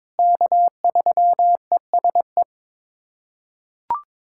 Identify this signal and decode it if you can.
Morse code, heard as a keyed tone, about 22 wpm, 700 Hz. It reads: K3ESE
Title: USA Callsign DE Callsign 22wpm